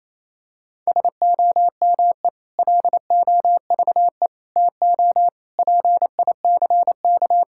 Morse code, heard as a keyed tone, about 28 wpm, 700 Hz. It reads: SOME LOVE TO PICK
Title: Sentences from Top 1600 Words 28wpm